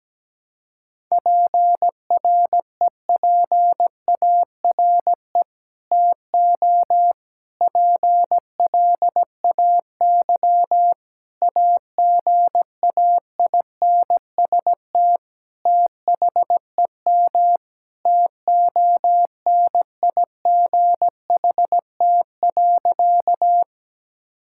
Morse code, heard as a keyed tone, about 17 wpm, 700 Hz. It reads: PREPARE TO PLAY AGAINST THEM TONIGHT.
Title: Sentences from Top 1000 Words - Review Part 6 - 17wpm